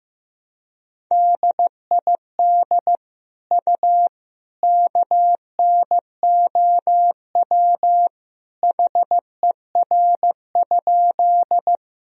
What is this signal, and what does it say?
Morse code, heard as a keyed tone, about 15 wpm, 700 Hz. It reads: DID U KNOW HER?
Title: Sentences from Top 100 Words and Abbreviations 15wpm